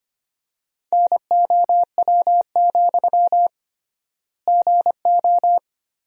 Morse code, heard as a keyed tone, about 25 wpm, 700 Hz. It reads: NOW, GO
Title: 3x Word-Spacing - Sentences from Top 200 Words - 25wpm